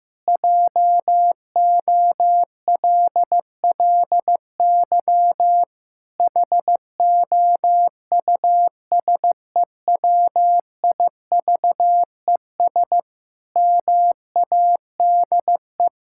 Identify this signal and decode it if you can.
Morse code, heard as a keyed tone, about 14 wpm, 700 Hz. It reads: JOLLY HOUSEWIVES MADE
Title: Pangram Challenge v2 - 15wpm